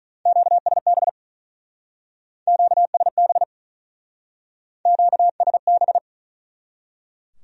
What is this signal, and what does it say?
Morse code, heard as a keyed tone, about 41 wpm, 700 Hz. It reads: QSB QSB QSB
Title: Speed-Racing Q-Codes - 35wpm